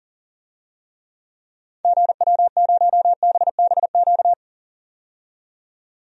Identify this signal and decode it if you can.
Morse code, heard as a keyed tone, about 40 wpm, 700 Hz. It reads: GW0BBQ
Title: DXCC-100 Call Signs - 6 Characters - 40wpm